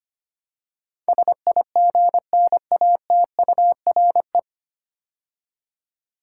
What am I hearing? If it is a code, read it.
Morse code, heard as a keyed tone, about 25 wpm, 700 Hz. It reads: SIGNATURE
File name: practice / 25wpm / 9 Letter Words - First 3 Letters Given - 25wpm